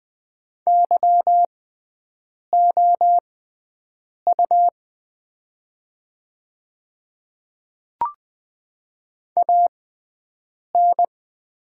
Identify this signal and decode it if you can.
Morse code, heard as a keyed tone, about 20 wpm, spaced as for 7 wpm, 700 Hz. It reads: YOU AN
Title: 6x Character Spacing - Top 100 Words - 20wpm